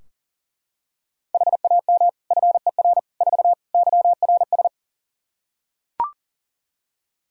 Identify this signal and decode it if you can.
Morse code, heard as a keyed tone, about 40 wpm, 700 Hz. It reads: HAM FER 4 YRS
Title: Practice QSO - Exchange 3 - More Info - 40wpm